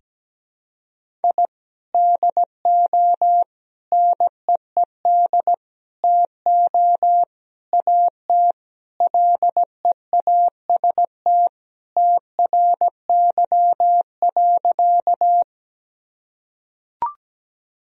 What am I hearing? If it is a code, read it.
Morse code, heard as a keyed tone, about 17 wpm, 700 Hz. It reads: I DO NEED TO AT LEAST TRY.
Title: Sentences from Top 1000 Words - Review Part 1 - 17wpm